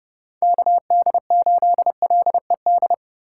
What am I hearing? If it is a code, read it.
Morse code, heard as a keyed tone, about 29 wpm, 700 Hz. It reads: KD8LED